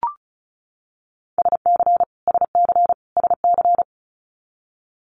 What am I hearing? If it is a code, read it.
Morse code, heard as a keyed tone, about 36 wpm, 700 Hz. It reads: SC SC SC
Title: Multi-letter Phonemes Repeated 3x 35wpm